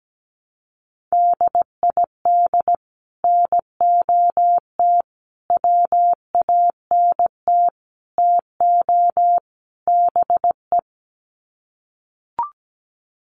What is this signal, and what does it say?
Morse code, heard as a keyed tone, about 17 wpm, 700 Hz. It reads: DID NOT WANT TO BE
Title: Sets of 5 Words 17wpm